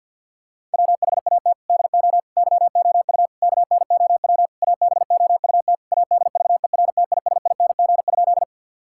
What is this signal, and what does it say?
Morse code, heard as a keyed tone, about 51 wpm, 700 Hz. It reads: WHAT DO YOU KNOW ABOUT ADVERTISING?
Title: Sentences from Top 1000 Words - Review Part 6 - 50wpm